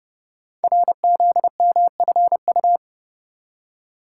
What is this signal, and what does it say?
Morse code, heard as a keyed tone, about 30 wpm, 700 Hz. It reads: RZMFU
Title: Code Groups - 5 Letter:Numbers - 30wpm